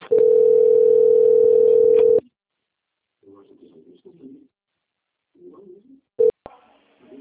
Ringback tone (North America), 440 + 480 Hz.
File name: out- - - - - .28418